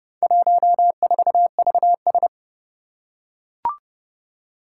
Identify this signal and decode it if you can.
Morse code, heard as a keyed tone, about 30 wpm, 700 Hz. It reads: 14VS
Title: Four Challenging Letter-Number Combinations 30wpm